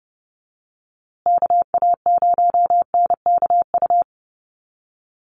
Morse code, heard as a keyed tone, about 30 wpm, 700 Hz. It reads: KA0NKU
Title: USA Call Signs Repeated 3x 30wpm